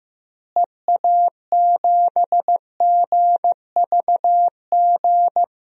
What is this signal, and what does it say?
Morse code, heard as a keyed tone, about 15 wpm, 700 Hz. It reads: EA7GVG
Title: DXCC-100 Call Signs - 6 Characters - Repeated 3x - 15wpm